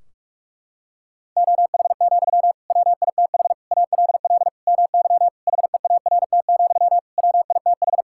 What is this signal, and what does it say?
Morse code, heard as a keyed tone, about 46 wpm, 700 Hz. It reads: OH, WITH ALL MY HEART, WITH
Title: Sentences from Top 1600 Words 45wpm